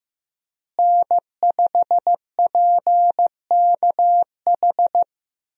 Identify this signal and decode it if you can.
Morse code, heard as a keyed tone, about 15 wpm, 700 Hz. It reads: N5PKH
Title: USA Call Signs - 5 Characters - Repeated 2x - 15wpm